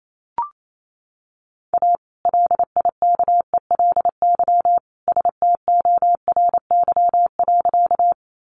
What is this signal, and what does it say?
Morse code, heard as a keyed tone, about 28 wpm, 700 Hz. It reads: A LIKELY STORY.
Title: Sentences from Top 600 Words 28wpm